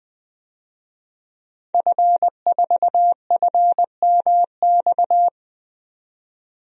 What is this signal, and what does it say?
Morse code, heard as a keyed tone, about 20 wpm, 700 Hz. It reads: F4FMX
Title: DXCC-100 Call Signs - 5 Characters - 20wpm